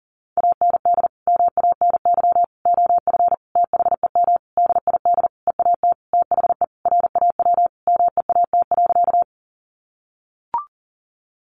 Morse code, heard as a keyed tone, about 40 wpm, 700 Hz. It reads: AND MANY OF THEM DID EAT THE RAW MEAT.
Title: Sentences from Top 1500 Words 40wpm